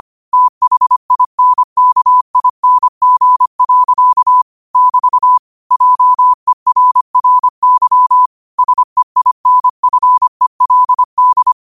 Morse code, heard as a keyed tone, about 25 wpm, 1000 Hz. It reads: THINKING. <BT> JERRY SEINFELD